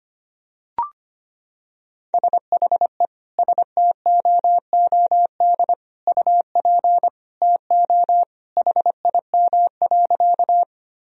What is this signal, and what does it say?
Morse code, heard as a keyed tone, about 25 wpm, 700 Hz. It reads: SHE STOOD UP TO HIM.
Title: Sentences from Top 1000 Words - Review Part 2 - 25wpm